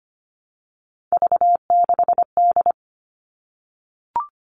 Morse code, heard as a keyed tone, about 25 wpm, 700 Hz. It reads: V6D
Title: All Cut Numbers - 3-Digit 25wpm